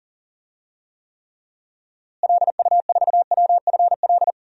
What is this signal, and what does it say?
Morse code, heard as a keyed tone, about 40 wpm, 700 Hz. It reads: LU4WFL
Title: DXCC-100 Call Signs - 6 Characters - Repeated 2x - 40wpm